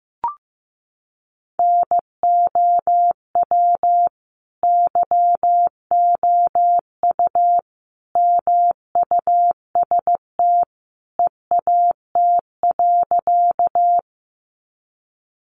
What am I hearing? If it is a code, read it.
Morse code, heard as a keyed tone, about 15 wpm, 700 Hz. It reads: NOW YOU MUST EAT.
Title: Sentences from Top 300 Words 15wpm